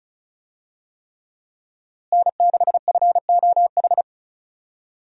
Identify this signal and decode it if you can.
Morse code, heard as a keyed tone, about 36 wpm, 700 Hz. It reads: N6FOH
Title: USA Call Signs - 5 Characters - Repeated 3x - 35wpm